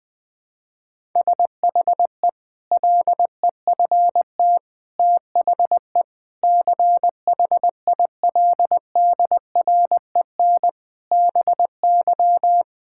Morse code, heard as a keyed tone, about 20 wpm, 700 Hz. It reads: SHE LEFT THE CHILDREN BY
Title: Sentences from Top 1000 Words - Review Part 5 - 20wpm